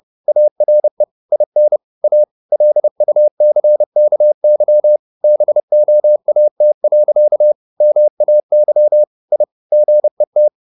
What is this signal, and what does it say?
Morse code, heard as a keyed tone, about 30 wpm, 600 Hz. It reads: ARE IN A LUCKY BOAT. MAY I GET
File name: the old man and the sea / the old man and the sea 1 30wpm